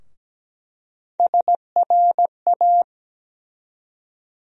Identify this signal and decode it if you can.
Morse code, heard as a keyed tone, about 17 wpm, 700 Hz. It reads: SRA
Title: USA Call Sign Postfixes - 3 Letters 17wpm